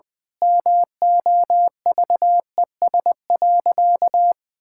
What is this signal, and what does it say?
Morse code, heard as a keyed tone, about 21 wpm, 700 Hz. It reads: MOVES.
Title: Pangram Challenge v2 - 20wpm